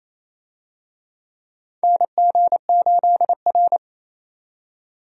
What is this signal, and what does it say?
Morse code, heard as a keyed tone, about 28 wpm, 700 Hz. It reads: NG8R
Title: USA Call Signs - 4 Characters - 28wpm